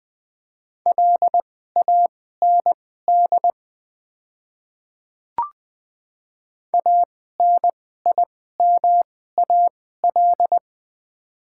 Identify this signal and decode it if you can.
Morse code, heard as a keyed tone, about 21 wpm, spaced as for 15 wpm, 700 Hz. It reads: LAND ANIMAL
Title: 2x Character Spacing - Top 300 Words - 20wpm